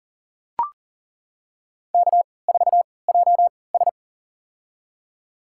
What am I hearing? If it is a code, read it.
Morse code, heard as a keyed tone, about 37 wpm, spaced as for 22 wpm, 700 Hz. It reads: K4JS
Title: USA Call Signs - 3x Character Spacing - 40wpm